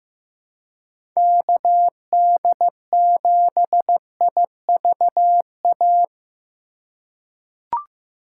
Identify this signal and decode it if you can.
Morse code, heard as a keyed tone, about 15 wpm, 700 Hz. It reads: KD7IVA
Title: USA Call Signs - 2x Character Spacing - 15wpm